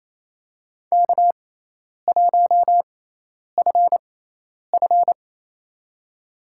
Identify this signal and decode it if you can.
Morse code, heard as a keyed tone, about 28 wpm, spaced as for 10 wpm, 700 Hz. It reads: K1FF
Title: USA Call Signs - 4 Characters - 6x Character Spacing - 28wpm